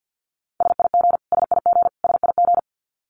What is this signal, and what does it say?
Morse code, heard as a keyed tone, about 50 wpm, 700 Hz. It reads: SIG SIG SIG
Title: Abbreviations Repeated 3x - 50wpm